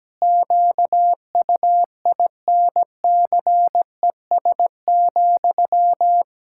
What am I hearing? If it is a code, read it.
Morse code, heard as a keyed tone, about 17 wpm, 700 Hz. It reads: QUINCES,
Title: Pangram Challenge v2 - 17wpm